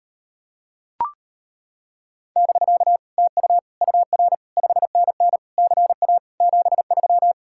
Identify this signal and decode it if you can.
Morse code, heard as a keyed tone, about 33 wpm, 700 Hz. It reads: <BK> TU UR 5NN CA 73